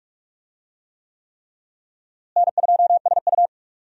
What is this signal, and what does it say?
Morse code, heard as a keyed tone, about 44 wpm, 700 Hz. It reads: N2SU
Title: USA Call Signs - 4 Characters - Repeated 3x - 45wpm